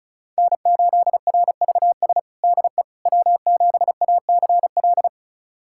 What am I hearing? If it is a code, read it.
Morse code, heard as a keyed tone, about 34 wpm, 700 Hz. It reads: N8RVS DE W7ACL
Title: USA Callsign DE Callsign - 5 Characters - 35wpm